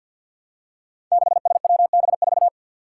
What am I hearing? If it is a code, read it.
Morse code, heard as a keyed tone, about 48 wpm, 700 Hz. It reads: 6SWB4
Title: Five Challenging Letter-Number Combinations 50wpm